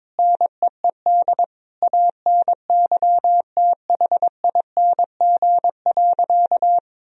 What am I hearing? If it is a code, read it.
Morse code, heard as a keyed tone, about 22 wpm, 700 Hz. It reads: NEED ANYTHING.
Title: Sentences from Top 1000 Words - Review Part 1 - 22wpm